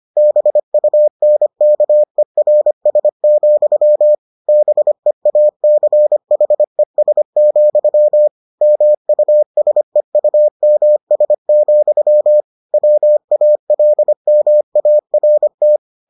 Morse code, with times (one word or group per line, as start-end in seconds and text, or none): 0.2-4.1 BUNKERS,
4.5-8.3 BEACHES,
8.6-12.4 MUSEUMS,
12.7-15.8 WALMART